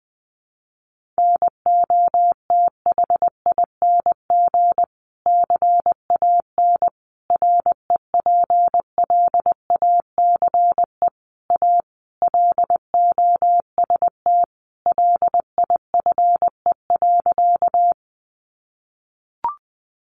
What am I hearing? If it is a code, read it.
Morse code, heard as a keyed tone, about 20 wpm, 700 Hz. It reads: NOTHING CAN REPLACE A LOST LIFE.
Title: Sentences from Top 1400 Words 20wpm